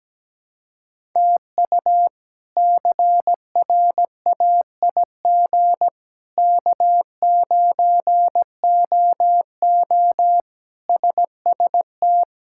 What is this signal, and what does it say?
Morse code, heard as a keyed tone, about 17 wpm, 700 Hz. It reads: TU CRAIG K9OO SST
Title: K1USN SST QSO Practice - 17wpm